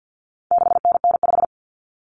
Morse code, heard as a keyed tone, about 50 wpm, 700 Hz. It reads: 6NN5
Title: Common Cut Numbers - 4-Digit 50wpm